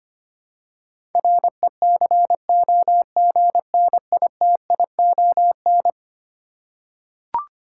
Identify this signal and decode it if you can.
Morse code, heard as a keyed tone, about 25 wpm, 700 Hz. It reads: RECOGNITION
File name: practice / 25wpm / Eleven Letter Words - 25wpm